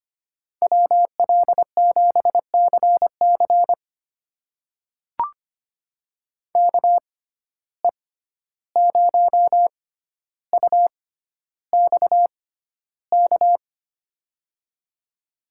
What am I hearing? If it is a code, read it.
Morse code, heard as a keyed tone, about 25 wpm, spaced as for 12 wpm, 700 Hz. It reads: WL7CC KE0UXK
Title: USA Call Signs - 6x Character Spacing - 25wpm